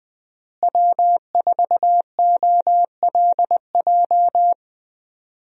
Morse code, heard as a keyed tone, about 21 wpm, 700 Hz. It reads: W4OLJ
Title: USA Call Signs - 5 Characters - Repeated 2x - 20wpm